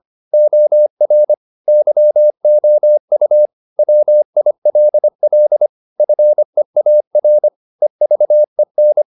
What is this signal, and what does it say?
Morse code, heard as a keyed tone, about 25 wpm, 600 Hz. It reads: OR YOU WILL FEAR EVEN
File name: the old man and the sea part 1 25wpm